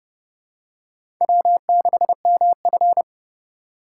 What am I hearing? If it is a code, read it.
Morse code, heard as a keyed tone, about 32 wpm, 700 Hz. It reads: W6MF
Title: USA Call Signs - 4 Characters - 4x Character Spacing - 30wpm